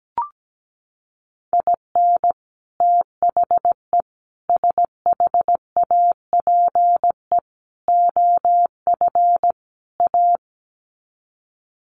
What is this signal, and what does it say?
Morse code, heard as a keyed tone, about 17 wpm, 700 Hz. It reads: IN THE SHAPE OF A
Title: Sets of 5 Words Encore 17wpm